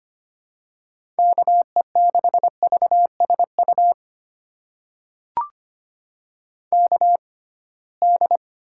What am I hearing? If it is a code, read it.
Morse code, heard as a keyed tone, about 25 wpm, spaced as for 19 wpm, 700 Hz. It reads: KE6VSU KD
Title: USA Call Signs - 6 Characters - 6x Character Spacing - 25wpm